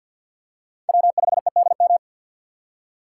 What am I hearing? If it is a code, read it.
Morse code, heard as a keyed tone, about 53 wpm, 700 Hz. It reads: W5EDM
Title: USA Call Signs - 5 Characters - 4x Character Spacing - 50wpm